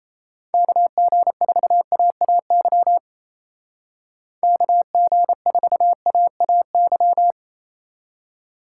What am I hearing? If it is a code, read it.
Morse code, heard as a keyed tone, about 34 wpm, 700 Hz. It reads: KG4AAY KG4AAY
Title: Speed Racing - USA 2x3 Call Signs 28wpm